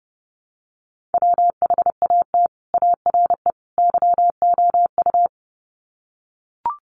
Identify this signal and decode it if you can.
Morse code, heard as a keyed tone, about 30 wpm, 700 Hz. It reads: WHAT ARE YOU
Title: Sets of 3 Words 30wpm